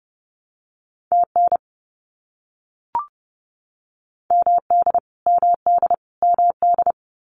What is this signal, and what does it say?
Morse code, heard as a keyed tone, about 30 wpm, 700 Hz. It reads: TN MD MD MD
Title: US States Repeated 3x 30wpm